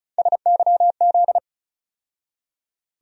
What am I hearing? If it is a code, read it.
Morse code, heard as a keyed tone, about 36 wpm, 700 Hz. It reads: SYZ
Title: USA Call Sign Postfixes - 3 Letters 35wpm